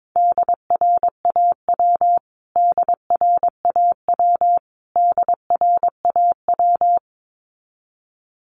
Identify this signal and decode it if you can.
Morse code, heard as a keyed tone, about 22 wpm, 700 Hz. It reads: DRAW DRAW DRAW
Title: Top 300 Words Repeated 22wpm